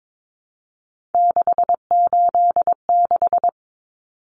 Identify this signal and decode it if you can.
Morse code, heard as a keyed tone, about 22 wpm, 700 Hz. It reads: 686
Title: Common Cut Numbers - 3-Digit 22wpm